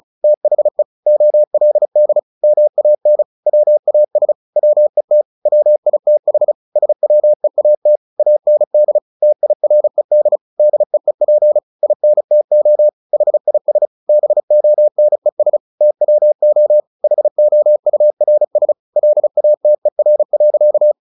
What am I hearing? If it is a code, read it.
Morse code, heard as a keyed tone, about 35 wpm, 600 Hz. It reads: THE OLD MAN WAS WET WITH SWEAT AND TIRED DEEP INTO HIS BONES TWO HOURS LATER.
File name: the old man and the sea 2 35wpm